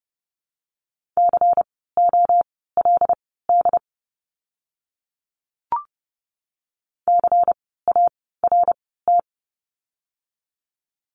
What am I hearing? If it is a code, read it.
Morse code, heard as a keyed tone, about 30 wpm, spaced as for 17 wpm, 700 Hz. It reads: COLD CART
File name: Lesson 16f - Letter c - Words 30wpm